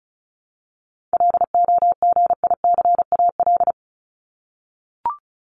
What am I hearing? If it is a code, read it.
Morse code, heard as a keyed tone, about 34 wpm, 700 Hz. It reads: LOGICAL
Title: Top 1501-1750 Words 35wpm